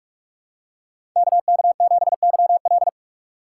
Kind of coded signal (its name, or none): Morse code